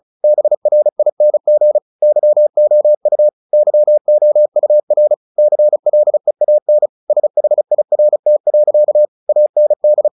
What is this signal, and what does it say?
Morse code, heard as a keyed tone, about 34 wpm, 600 Hz. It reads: BRING YOU YOUR CLEAN SHIRT. AND
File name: the old man and the sea 2 35wpm